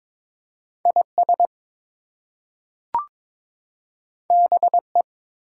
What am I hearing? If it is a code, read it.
Morse code, heard as a keyed tone, about 22 wpm, 700 Hz. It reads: IS BE
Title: Two Letter Words - 22wpm